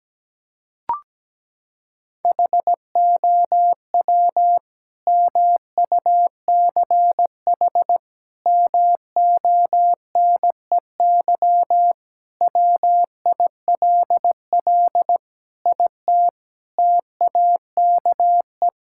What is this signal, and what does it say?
Morse code, heard as a keyed tone, about 17 wpm, 700 Hz. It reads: HOW MUCH MONEY WILL IT TAKE
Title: Sentences from Top 1000 Words - Review Part 5 - 17wpm